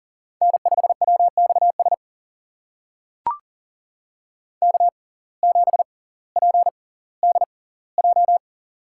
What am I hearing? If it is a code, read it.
Morse code, heard as a keyed tone, about 40 wpm, spaced as for 20 wpm, 700 Hz. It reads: N5WXS K7PDJ
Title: USA Call Signs - 5 Characters - 6x Character Spacing - 40wpm